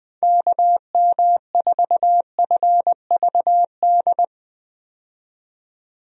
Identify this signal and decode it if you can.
Morse code, heard as a keyed tone, about 20 wpm, 700 Hz. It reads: KM4FVD
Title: USA Call Signs - 6 Characters - 20wpm